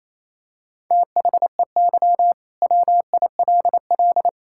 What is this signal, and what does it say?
Morse code, heard as a keyed tone, about 28 wpm, 700 Hz. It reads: THEY WILL
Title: Sentences from Top 1000 Words - Review Part 5 - 28wpm